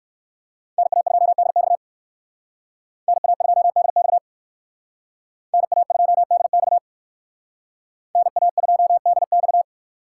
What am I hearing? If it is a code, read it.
Morse code, heard as a keyed tone, about 57 wpm, 700 Hz. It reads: NA2DX NA2DX NA2DX NA2DX